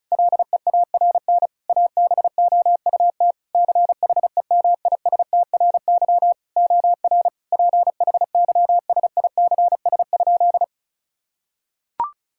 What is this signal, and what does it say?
Morse code, heard as a keyed tone, about 36 wpm, 700 Hz. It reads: LEARN ABOUT CHEMISTRY OR PHYSICS?
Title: Sentences from Top 1000 Words - Review Part 6 - 35wpm